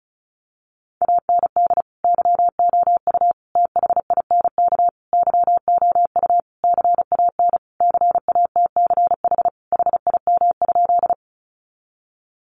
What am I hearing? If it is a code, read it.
Morse code, heard as a keyed tone, about 34 wpm, 700 Hz. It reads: AND YOU THINK YOU CAN CATCH HIM?
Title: Sentences from Top 1500 Words 35wpm